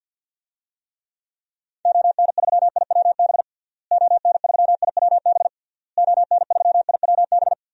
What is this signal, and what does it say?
Morse code, heard as a keyed tone, about 52 wpm, 700 Hz. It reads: ON3IWB ON3IWB ON3IWB
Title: DXCC-100 Call Signs - 6 Characters - Repeated 3x - 50wpm